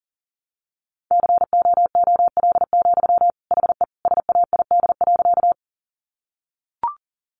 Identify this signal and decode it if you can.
Morse code, heard as a keyed tone, about 37 wpm, 700 Hz. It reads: COOL, HE SAID.
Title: Sentences from Top 1200 Words 40wpm